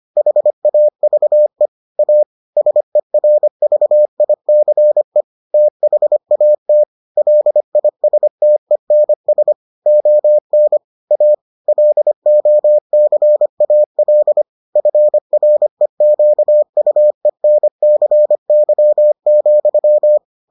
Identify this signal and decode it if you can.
Morse code, heard as a keyed tone, about 25 wpm, 600 Hz. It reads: HAVE A SERVICE THAT LISTENS ON A LOCAL FREQUENCY,